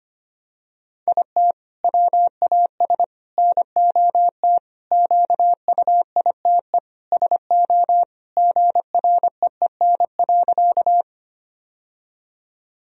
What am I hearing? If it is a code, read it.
Morse code, heard as a keyed tone, about 25 wpm, 700 Hz. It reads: IT WAS NOT QUITE SO GREEN.
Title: Sentences from Top 1000 Words - Review Part 2 - 25wpm